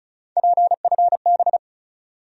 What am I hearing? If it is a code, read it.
Morse code, heard as a keyed tone, about 36 wpm, 700 Hz. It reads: PFB
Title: USA Call Sign Postfixes - 3 Letters 35wpm